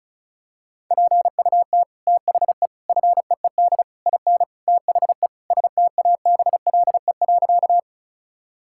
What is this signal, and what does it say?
Morse code, heard as a keyed tone, about 34 wpm, 700 Hz. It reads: PUT THE FEED IN THE STABLE.
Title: Sentences from Top 1600 Words 35wpm